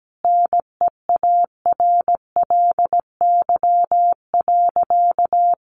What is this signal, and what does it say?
Morse code, heard as a keyed tone, about 17 wpm, 700 Hz. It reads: NEARLY.
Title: Sentences from Top 1300 Words 17wpm